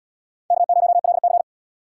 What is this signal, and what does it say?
Morse code, heard as a keyed tone, about 72 wpm, 700 Hz. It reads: K0LC